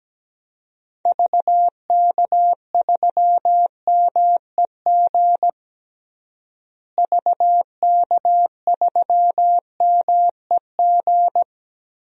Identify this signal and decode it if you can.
Morse code, heard as a keyed tone, about 17 wpm, 700 Hz. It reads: VK3MEG VK3MEG
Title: DXCC-100 Call Signs - 6 Characters - Repeated 3x - 17wpm